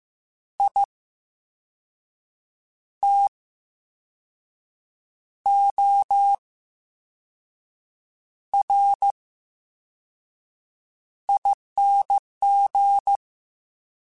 Morse code, heard as a keyed tone, about 15 wpm, spaced as for 5 wpm, 800 Hz. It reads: ITORING
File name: Special INGs / Special INGs-mp3s / Slow ing @13